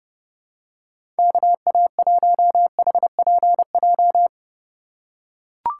Morse code, heard as a keyed tone, about 30 wpm, 700 Hz. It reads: KA1HPJ
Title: USA Call Signs - 6 Characters - 3x Character Spacing - 30wpm